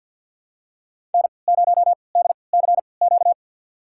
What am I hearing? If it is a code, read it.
Morse code, heard as a keyed tone, about 53 wpm, spaced as for 29 wpm, 700 Hz. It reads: N0DCQ
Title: USA Call Signs - 5 Characters - 3x Character Spacing - 50wpm